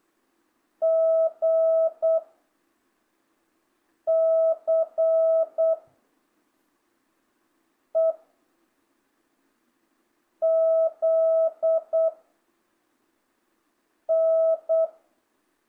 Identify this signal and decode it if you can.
Morse code, heard as a keyed tone, about 8 wpm, 660 Hz. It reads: G C E Z N
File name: jh8cbh / 2025cw